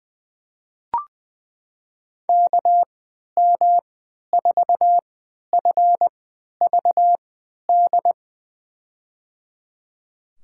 Morse code, heard as a keyed tone, about 21 wpm, spaced as for 12 wpm, 700 Hz. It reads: KM4FVD
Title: USA Call Signs - 6 Characters - 3x Character Spacing - 20wpm